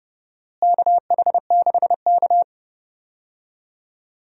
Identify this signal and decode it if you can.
Morse code, heard as a keyed tone, about 30 wpm, 700 Hz. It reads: KH6K
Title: DXCC-100 Call Signs - 4 Characters - Repeated 3x - 30wpm